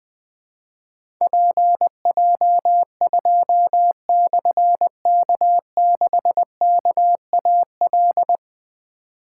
Morse code, heard as a keyed tone, about 20 wpm, 700 Hz. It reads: PJ2/K6KAL
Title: Difficult Call Signs 20wpm